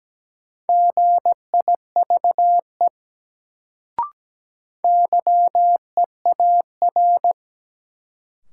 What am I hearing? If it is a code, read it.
Morse code, heard as a keyed tone, about 17 wpm, 700 Hz. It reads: GIVE YEAR
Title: Sets of 2 Words - New Word Drill - 17wpm